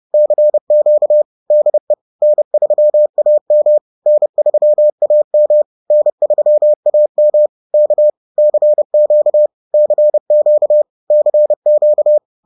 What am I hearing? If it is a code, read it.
Morse code, heard as a keyed tone, about 30 wpm, 600 Hz. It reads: CQ DE N3AM N3AM N3AM K CQ CQ CQ
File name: 30wpm-4